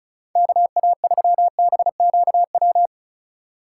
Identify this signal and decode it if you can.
Morse code, heard as a keyed tone, about 34 wpm, 700 Hz. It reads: KA3BQW
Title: USA Call Signs - 6 Characters - 2x Character Spacing - 35wpm